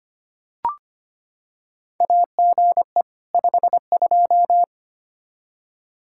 Morse code, heard as a keyed tone, about 25 wpm, 700 Hz. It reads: AGE 52
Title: Practice QSO - Exchange 3 - More Info - 25wpm